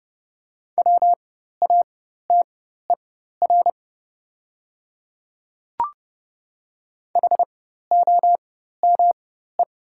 Morse code, heard as a keyed tone, about 30 wpm, spaced as for 14 wpm, 700 Hz. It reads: WATER HOME
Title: 4x Character Spacing - Top 300 Words - 30wpm